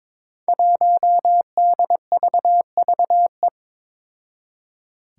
Morse code, heard as a keyed tone, about 22 wpm, 700 Hz. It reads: 1DVVE
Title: Five Challenging Letter-Number Combinations 22wpm